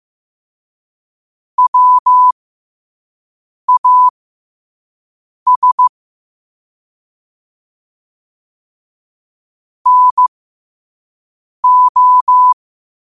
Morse code, heard as a keyed tone, about 15 wpm, spaced as for 5 wpm, 1000 Hz. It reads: WAS NO
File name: qotdfiles / 5wpm